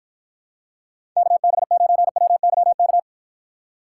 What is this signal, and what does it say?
Morse code, heard as a keyed tone, about 51 wpm, 700 Hz. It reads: KB9WYK